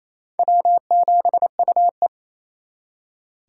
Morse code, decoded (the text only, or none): W7UE